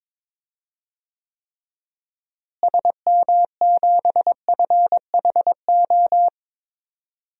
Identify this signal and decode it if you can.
Morse code, heard as a keyed tone, about 22 wpm, 700 Hz. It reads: SM7FHO
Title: DXCC-100 Call Signs - 6 Characters - Repeated 3x - 22wpm